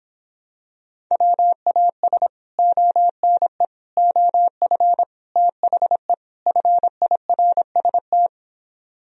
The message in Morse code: WAS ONE OF THE FIRST

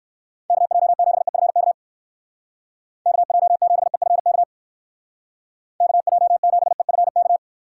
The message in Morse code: KJ7EFK KJ7EFK KJ7EFK

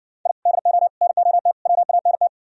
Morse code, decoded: I DO NOT WANT

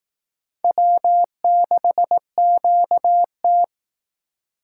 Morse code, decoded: W6QT

